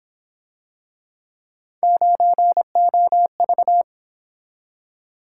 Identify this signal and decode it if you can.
Morse code, heard as a keyed tone, about 26 wpm, 700 Hz. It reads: 9OV
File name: Speed-Racing - Three Letter-Number Combinations - 22wpm